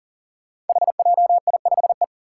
Morse code, heard as a keyed tone, about 40 wpm, 700 Hz. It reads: HJI5E
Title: Five Challenging Letter-Number Combinations 40wpm